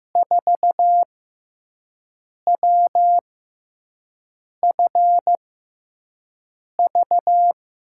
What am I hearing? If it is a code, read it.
Morse code, heard as a keyed tone, about 15 wpm, spaced as for 5 wpm, 700 Hz. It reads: 4WFV
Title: USA Call Signs - 5 Characters - 6x Character Spacing - 15wpm